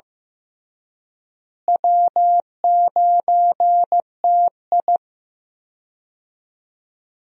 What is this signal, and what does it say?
Morse code, heard as a keyed tone, about 15 wpm, 700 Hz. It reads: W9TI
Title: USA Call Signs - 4 Characters - Repeated 3x - 15wpm